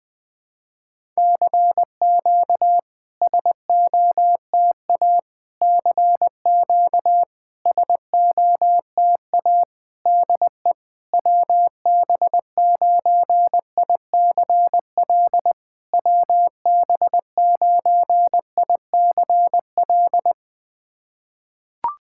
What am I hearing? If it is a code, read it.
Morse code, heard as a keyed tone, about 21 wpm, 660 Hz. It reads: CQ SOTA CQ SOTA DE WB9ICL WB9ICL
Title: SOTA QSO Practice 20wpm